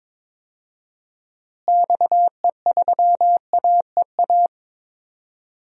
Morse code, heard as a keyed tone, about 22 wpm, 700 Hz. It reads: XE3AEA
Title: DXCC-100 Call Signs - 6 Characters - Repeated 3x - 22wpm